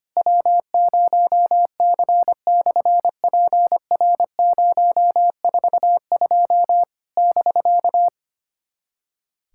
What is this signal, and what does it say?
Morse code, heard as a keyed tone, about 25 wpm, 700 Hz. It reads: W0C/PR042 <BK>
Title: SOTA QSO Practice 25wpm